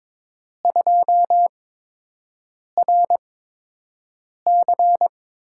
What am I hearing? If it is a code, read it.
Morse code, heard as a keyed tone, about 22 wpm, spaced as for 6 wpm, 700 Hz. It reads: 2RC
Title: USA Call Signs - 5 Characters - 8x Character Spacing - 22wpm